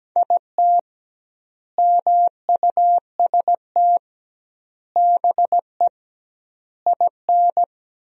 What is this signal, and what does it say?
Morse code, heard as a keyed tone, about 17 wpm, 700 Hz. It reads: IT MUST BE IN
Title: 2x Word-Spacing - Sentences from Top 300 Words - 17wpm